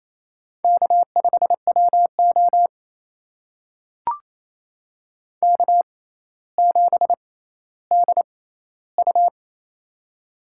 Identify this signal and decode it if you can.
Morse code, heard as a keyed tone, about 28 wpm, spaced as for 14 wpm, 700 Hz. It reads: K5WO K7DU
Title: USA Call Signs - 4 Characters - 6x Character Spacing - 28wpm